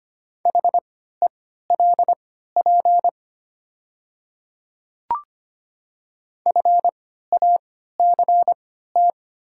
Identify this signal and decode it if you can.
Morse code, heard as a keyed tone, about 25 wpm, spaced as for 14 wpm, 700 Hz. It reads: HELP FACT
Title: 3x Character Spacing - Top 500 Words - 25wpm